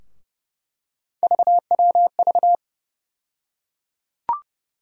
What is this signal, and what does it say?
Morse code, heard as a keyed tone, about 29 wpm, 700 Hz. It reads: VWV
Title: USA Call Sign Postfixes - 3 Letters 30wpm